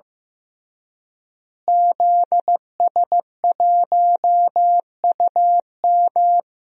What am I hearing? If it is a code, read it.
Morse code, heard as a keyed tone, about 15 wpm, 700 Hz. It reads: ZS1UM
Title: DXCC-100 Call Signs - 5 Characters - Repeated 3x - 15wpm